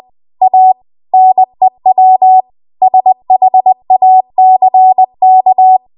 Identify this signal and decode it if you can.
Morse code, heard as a keyed tone, about 20 wpm, 750 Hz. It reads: A NEW SHACK
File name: Morse / 20WPM